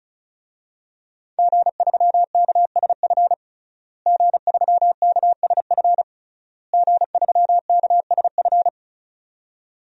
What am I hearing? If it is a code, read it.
Morse code, heard as a keyed tone, about 36 wpm, 700 Hz. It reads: G3KSF G3KSF G3KSF
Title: DXCC-100 Call Signs - 5 Characters - Repeated 3x - 35wpm